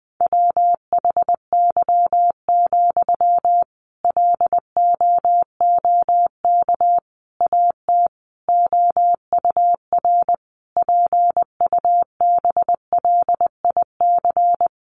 Morse code, heard as a keyed tone, about 20 wpm, 700 Hz. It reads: WHY, LOOK AT OUR PUBLIC
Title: Sentences from Top 600 Words 20wpm